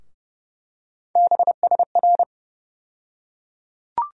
Morse code, heard as a keyed tone, about 32 wpm, 700 Hz. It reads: BSR
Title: USA Call Sign Postfixes - 3 Letters 30wpm